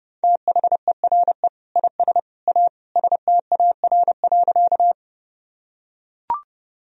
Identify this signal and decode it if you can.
Morse code, heard as a keyed tone, about 32 wpm, 700 Hz. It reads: THERE IS A STAR.
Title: Sentences from Top 1000 Words - Review Part 2 - 30wpm